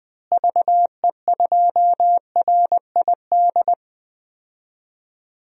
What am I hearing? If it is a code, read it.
Morse code, heard as a keyed tone, about 20 wpm, 700 Hz. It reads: VE2RID
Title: DXCC-100 Call Signs - 6 Characters - Repeated 3x - 20wpm